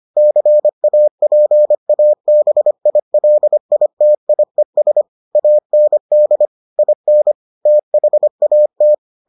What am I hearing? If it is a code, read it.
Morse code, heard as a keyed tone, about 25 wpm, 600 Hz. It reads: CAPABILITIES AND IN THAT